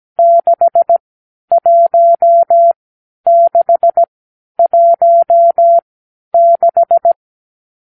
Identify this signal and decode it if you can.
Morse code, heard as a keyed tone, about 17 wpm, spaced as for 10 wpm, 680 Hz. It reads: 61616
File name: k7qo / Lesson 50 (1-6)